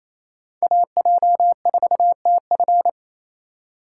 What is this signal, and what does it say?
Morse code, heard as a keyed tone, about 28 wpm, 700 Hz. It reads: AJ4TF
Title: K1USN SST QSO Practice - 28wpm